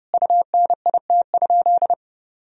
Morse code, heard as a keyed tone, about 32 wpm, 700 Hz. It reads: UNIT?
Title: Sentences from Top 1000 Words - Review Part 2 - 30wpm